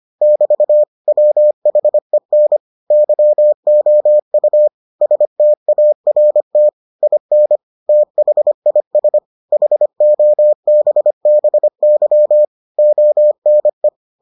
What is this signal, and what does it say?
Morse code, heard as a keyed tone, about 25 wpm, 600 Hz. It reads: <BT> WHEN YOU START IN THIS HOBBY ONE